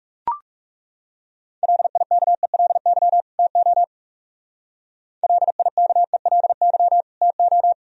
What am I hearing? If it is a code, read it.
Morse code, heard as a keyed tone, about 36 wpm, 700 Hz. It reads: LIKELY TO LIKELY TO
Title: Speed-Racing - Sets of 2 Words 30wpm